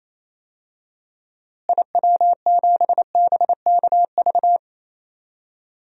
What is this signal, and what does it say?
Morse code, heard as a keyed tone, about 28 wpm, 700 Hz. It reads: IW7BKV
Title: DXCC-100 Call Signs - 6 Characters - Repeated 3x - 28wpm